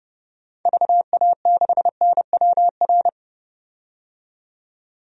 Morse code, heard as a keyed tone, about 30 wpm, 700 Hz. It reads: VA6NWR
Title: DXCC-100 Call Signs - 6 Characters - 30wpm